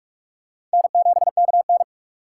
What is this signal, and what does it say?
Morse code, heard as a keyed tone, about 39 wpm, 700 Hz. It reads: N7KN